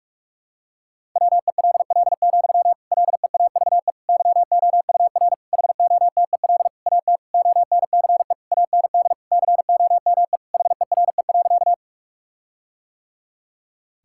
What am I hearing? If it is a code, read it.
Morse code, heard as a keyed tone, about 46 wpm, 700 Hz. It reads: WELL, LEAVE YOUR HOTEL AT ONCE AND COME HERE.
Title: Sentences from Top 1600 Words 45wpm